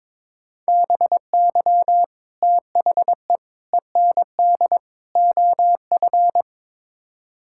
Morse code, heard as a keyed tone, about 22 wpm, 700 Hz. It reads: BY THE END OF
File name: practice / 22wpm / Build A Phrase - 5-Word Phrases - 22wpm